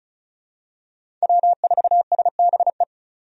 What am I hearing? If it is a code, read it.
Morse code, heard as a keyed tone, about 36 wpm, 700 Hz. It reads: W4SBE
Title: USA Call Signs - 5 Characters - 35wpm